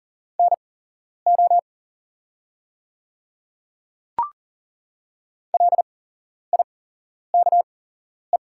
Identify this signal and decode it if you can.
Morse code, heard as a keyed tone, about 40 wpm, spaced as for 11 wpm, 700 Hz. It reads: NO LIKE
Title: 8x Character Spacing - Top 100 Words - 40wpm